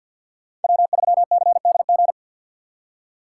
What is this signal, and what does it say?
Morse code, heard as a keyed tone, about 52 wpm, 700 Hz. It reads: W3CDG
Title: USA Call Signs - 5 Characters - 4x Character Spacing - 50wpm